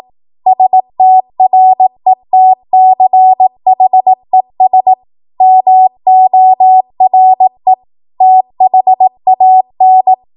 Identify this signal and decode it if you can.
Morse code, heard as a keyed tone, about 18 wpm, 750 Hz. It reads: STRETCHES MORE THAN
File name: Morse / 18WPM